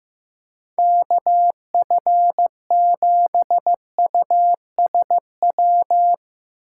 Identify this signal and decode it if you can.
Morse code, heard as a keyed tone, about 16 wpm, 700 Hz. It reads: KF7USW
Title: USA Call Signs - 6 Characters - 4x Character Spacing - 15wpm